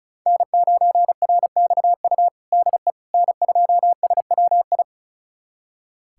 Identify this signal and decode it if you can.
Morse code, heard as a keyed tone, about 34 wpm, 700 Hz. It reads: N9RXU DE N2SWI
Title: USA Callsign DE Callsign - 5 Characters - 35wpm